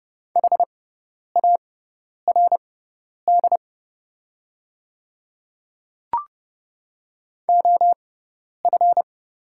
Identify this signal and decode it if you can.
Morse code, heard as a keyed tone, about 30 wpm, spaced as for 10 wpm, 700 Hz. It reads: HARD OF
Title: 6x Character Spacing - Top 500 Words - 30wpm